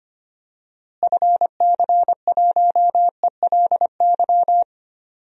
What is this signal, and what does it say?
Morse code, heard as a keyed tone, about 25 wpm, 700 Hz. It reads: FC1ELY
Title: DXCC-100 Call Signs - 6 Characters - Repeated 2x - 25wpm